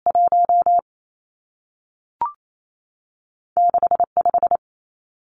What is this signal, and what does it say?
Morse code, heard as a keyed tone, about 28 wpm, 700 Hz. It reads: 1 6 5
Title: Numbers - 2-Digit 28wpm